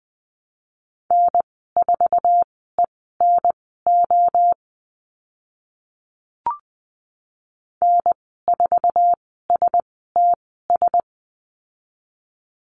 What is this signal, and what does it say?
Morse code, heard as a keyed tone, about 20 wpm, spaced as for 14 wpm, 700 Hz. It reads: N4ENO N4STS
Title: 048 Lesson 9g - Number 4 - Call Signs 20wpm